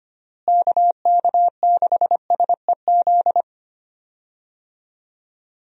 Morse code, heard as a keyed tone, about 25 wpm, 700 Hz. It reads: KK6SEZ
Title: USA Call Signs - 6 Characters - Repeated 2x - 25wpm